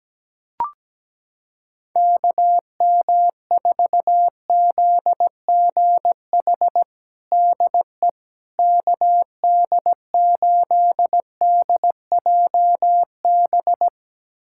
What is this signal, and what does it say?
Morse code, heard as a keyed tone, about 17 wpm, 700 Hz. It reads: KM4ZGH DE KD8DJB
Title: USA Callsign DE Callsign 17wpm